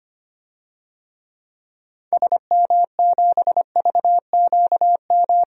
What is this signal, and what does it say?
Morse code, heard as a keyed tone, about 25 wpm, 700 Hz. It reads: SM7VQM
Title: DXCC-100 Call Signs - 6 Characters - Repeated 2x - 25wpm